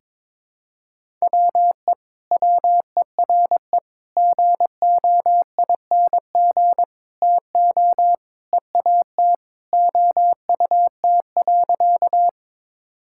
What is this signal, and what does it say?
Morse code, heard as a keyed tone, about 22 wpm, 700 Hz. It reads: WE WERE GOING TO EAT OUT.
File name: practice / 22wpm / Sentences with 4-Word Phrases - 22wpm